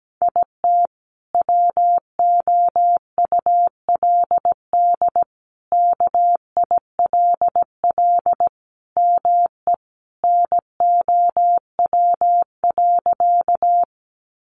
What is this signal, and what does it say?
Morse code, heard as a keyed tone, about 17 wpm, 700 Hz. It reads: IT WOULD KILL ME NOW.
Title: Sentences from Top 1400 Words 17wpm